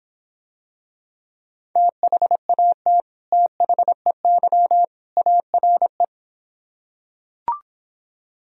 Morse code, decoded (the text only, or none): THAT THEY ARE